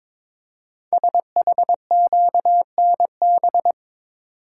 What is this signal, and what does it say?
Morse code, heard as a keyed tone, about 22 wpm, 700 Hz. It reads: SHQNB
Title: Code Groups - 5 Letters - 22wpm